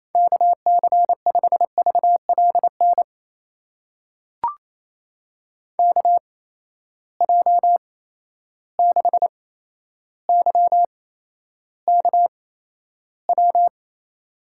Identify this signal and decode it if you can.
Morse code, heard as a keyed tone, about 28 wpm, spaced as for 12 wpm, 700 Hz. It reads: KC5VLN KJ6YKW
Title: USA Call Signs - 6 Characters - 8x Character Spacing - 28wpm